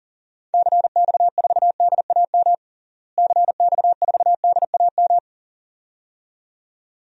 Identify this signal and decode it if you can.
Morse code, heard as a keyed tone, about 40 wpm, 700 Hz. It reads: CX4DAM CX4DAM
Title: DXCC-100 Call Signs - 6 Characters - Repeated 2x - 40wpm